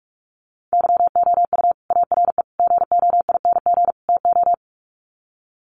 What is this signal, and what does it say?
Morse code, heard as a keyed tone, about 44 wpm, 700 Hz. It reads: YOU ARE GOING TO
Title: Sets of 4 Words 45wpm